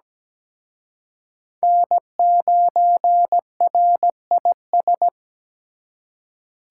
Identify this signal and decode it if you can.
Morse code, heard as a keyed tone, about 20 wpm, 700 Hz. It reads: N9RIS
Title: Speed Racing - USA Call Signs 17wpm